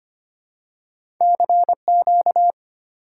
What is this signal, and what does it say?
Morse code, heard as a keyed tone, about 25 wpm, 690 Hz. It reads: CQ
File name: Top 90 QSO Elements - 3x Character Spacing - 25wpm